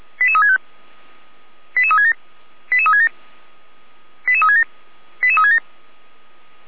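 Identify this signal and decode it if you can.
Five-tone selective call (ZVEI), heard as quick runs of five tones, tones of about 70 ms. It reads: 88365 88367 88367 88367 88367
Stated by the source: ZVEI five-tone call received by SDR.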